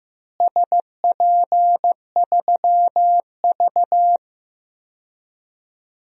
Morse code, heard as a keyed tone, about 15 wpm, 700 Hz. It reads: SP3V
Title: DXCC-100 Call Signs - 4 Characters - Repeated 3x - 15wpm